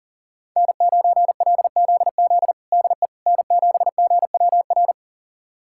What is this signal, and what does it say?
Morse code, heard as a keyed tone, about 40 wpm, 700 Hz. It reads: N9LZZ DE N7GWR
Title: USA Callsign DE Callsign - 5 Characters - 40wpm